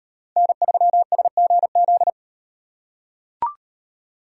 Morse code, heard as a keyed tone, about 35 wpm, 700 Hz. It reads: N3SGZ